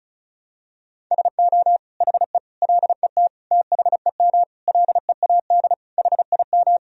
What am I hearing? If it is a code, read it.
Morse code, heard as a keyed tone, about 34 wpm, 700 Hz. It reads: SO HE LET THEM LEAD HIM
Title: Sentences from Top 1000 Words - Review Part 2 - 35wpm